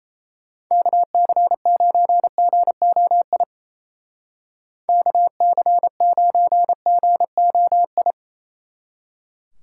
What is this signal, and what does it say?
Morse code, heard as a keyed tone, about 35 wpm, 700 Hz. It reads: KC9GOI KC9GOI